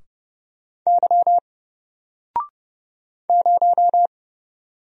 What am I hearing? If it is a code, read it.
Morse code, heard as a keyed tone, about 30 wpm, 700 Hz. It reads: Y 0
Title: Single Character (Letter:Number:Punctuation:Prosign) 30wpm